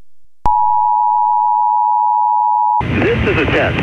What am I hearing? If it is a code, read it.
Emergency Alert System (SAME) tones (the 853 and 960 Hz attention tone), with speech. It is heard from 0.4 to 2.8 s.